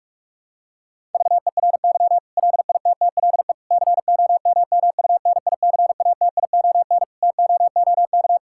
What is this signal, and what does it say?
Morse code, heard as a keyed tone, about 46 wpm, 700 Hz. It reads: VERY LITTLE COMMUNICATION TOOK
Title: Sentences from Top 1000 Words - Review Part 6 - 45wpm